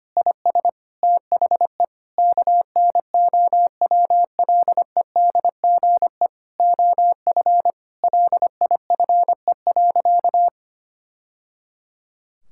Morse code, heard as a keyed tone, about 25 wpm, 700 Hz. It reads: IS THE KNOWLEDGE OF LIFE.
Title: Sentences from Top 1000 Words - Review Part 6 - 25wpm